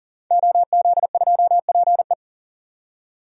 Morse code, heard as a keyed tone, about 40 wpm, 700 Hz. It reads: OZ2PE
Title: DXCC-100 Call Signs - 5 Characters - 40wpm